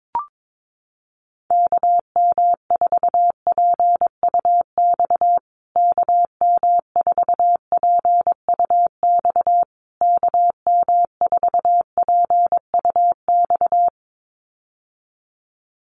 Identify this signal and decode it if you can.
Morse code, heard as a keyed tone, about 22 wpm, 700 Hz. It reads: KM4PUX KM4PUX KM4PUX
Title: USA Call Signs Repeated 3x 22wpm